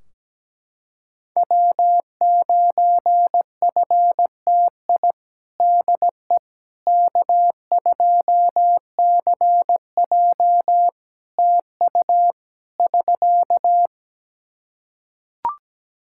Morse code, heard as a keyed tone, about 17 wpm, 700 Hz. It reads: W9FTI DE K2CJ TU <SK>
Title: K1USN SST QSO Practice 17wpm